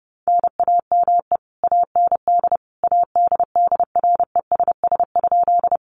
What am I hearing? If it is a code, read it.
Morse code, heard as a keyed tone, about 30 wpm, 700 Hz. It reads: NAME AND ADDRESS?
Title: Sentences from Top 1000 Words 30wpm